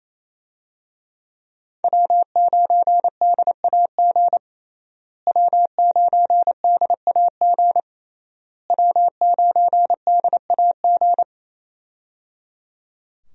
Morse code, heard as a keyed tone, about 28 wpm, 700 Hz. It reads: W9DAG W9DAG W9DAG
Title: USA Call Signs - 5 Characters - Repeated 3x - 28wpm